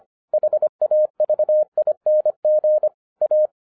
Morse code, heard as a keyed tone, about 25 wpm, 610 Hz. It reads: HAVING A